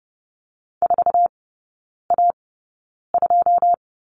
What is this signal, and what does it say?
Morse code, heard as a keyed tone, about 30 wpm, spaced as for 9 wpm, 700 Hz. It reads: 4A2
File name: course / 30:10wpm / 087 Lesson 15e - Number 5 - Three Characters 30wpm